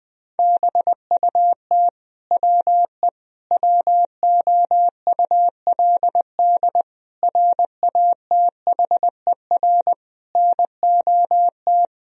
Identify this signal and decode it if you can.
Morse code, heard as a keyed tone, about 20 wpm, 700 Hz. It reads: BUT WE WOULD RATHER NOT
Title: Sentences from Top 1000 Words - Review Part 4 - 20wpm